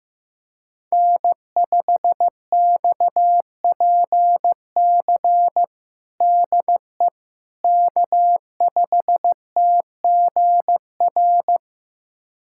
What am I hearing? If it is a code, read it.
Morse code, heard as a keyed tone, about 15 wpm, 700 Hz. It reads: N5XPC DE K5TGR
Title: USA Callsign DE Callsign - 5 Characters - 15wpm